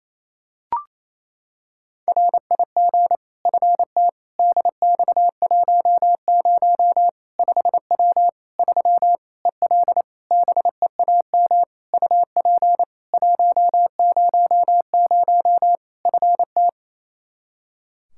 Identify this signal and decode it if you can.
Morse code, heard as a keyed tone, about 28 wpm, 700 Hz. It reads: RIG FT DX10 5W 3 EL BEAM UP 100 FT
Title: Practice QSO - Exchange 2 - Info - 28wpm